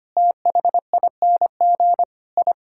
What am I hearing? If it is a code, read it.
Morse code, heard as a keyed tone, about 25 wpm, 700 Hz. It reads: THING I
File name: practice / 25wpm / Sentences with 3-Word Phrases - 25wpm